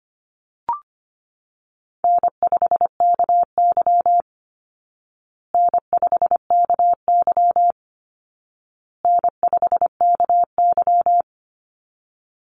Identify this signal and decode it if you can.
Morse code, heard as a keyed tone, about 25 wpm, 700 Hz. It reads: N5KY N5KY N5KY
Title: Easy Call Signs Repeated 3x - 25wpm